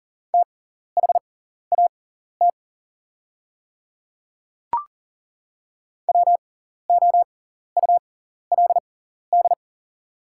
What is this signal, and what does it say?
Morse code, heard as a keyed tone, about 40 wpm, spaced as for 14 wpm, 690 Hz. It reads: THAT WOULD
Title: 6x Character Spacing - Top 100 Words - 40wpm